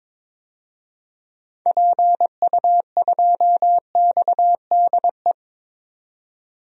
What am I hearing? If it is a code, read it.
Morse code, heard as a keyed tone, about 22 wpm, 700 Hz. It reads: PU2XDE
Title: DXCC-100 Call Signs - 6 Characters - 22wpm